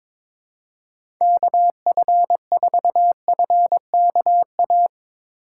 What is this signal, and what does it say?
Morse code, heard as a keyed tone, about 22 wpm, 700 Hz. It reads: KF4FKA
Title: USA Call Signs - 6 Characters - 2x Character Spacing - 22wpm